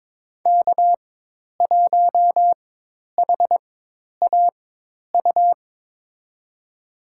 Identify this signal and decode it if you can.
Morse code, heard as a keyed tone, about 22 wpm, spaced as for 10 wpm, 700 Hz. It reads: K1HAU
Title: USA Call Signs - 5 Characters - 4x Character Spacing - 22wpm